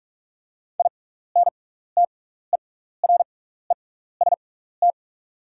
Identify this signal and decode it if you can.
Morse code, heard as a keyed tone, about 44 wpm, spaced as for 15 wpm, 690 Hz. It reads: INTEREST
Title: 6x Character Spacing - Top 500 Words - 45wpm